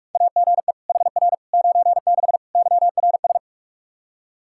Morse code, heard as a keyed tone, about 47 wpm, 700 Hz. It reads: AGE HR 96 YRS
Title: K1USN SST QSO Practice 45wpm